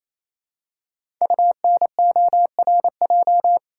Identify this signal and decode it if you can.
Morse code, heard as a keyed tone, about 28 wpm, 700 Hz. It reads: UNORJ
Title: Code Groups - 5 Letters - 28wpm